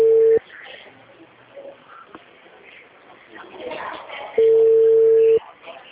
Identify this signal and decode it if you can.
Ringback tone (China, 450 Hz), with speech.
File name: out- -2020